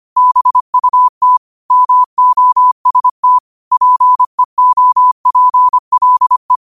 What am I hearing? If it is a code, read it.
Morse code, heard as a keyed tone, about 25 wpm, 1000 Hz. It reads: BUT MOST PEOPLE